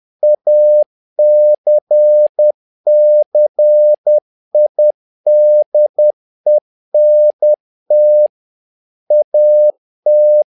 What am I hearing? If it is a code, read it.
Morse code, heard as a keyed tone, about 10 wpm, 600 Hz. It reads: ACCIDENT AT
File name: learning center / tower trouble / tower trouble 10wpm